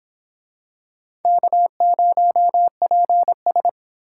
Morse code, heard as a keyed tone, about 26 wpm, 700 Hz. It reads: K0PS